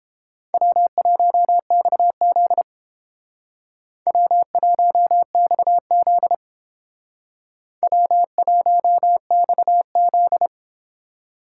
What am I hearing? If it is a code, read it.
Morse code, heard as a keyed tone, about 31 wpm, 700 Hz. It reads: W1XZ W1XZ W1XZ